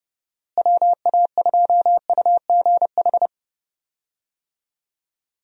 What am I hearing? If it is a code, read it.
Morse code, heard as a keyed tone, about 31 wpm, 700 Hz. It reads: WA2UGH
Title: USA Call Signs - 6 Characters - Repeated 2x - 30wpm